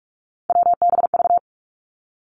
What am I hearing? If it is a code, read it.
Morse code, heard as a keyed tone, about 46 wpm, 700 Hz. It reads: WBV
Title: Three Challenging Letter-Number Combinations 45wpm